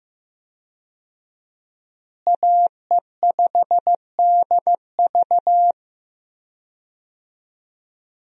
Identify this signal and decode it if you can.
Morse code, heard as a keyed tone, about 15 wpm, 700 Hz. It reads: AE5DV
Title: USA Call Signs - 5 Characters - Repeated 2x - 15wpm